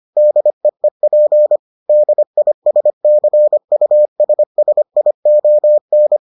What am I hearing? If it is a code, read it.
Morse code, heard as a keyed tone, about 25 wpm, 600 Hz. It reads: DEEP DISCUSSION